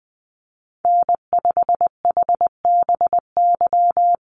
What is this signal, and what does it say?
Morse code, heard as a keyed tone, about 20 wpm, 700 Hz. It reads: N5HBY
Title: USA Call Signs 20wpm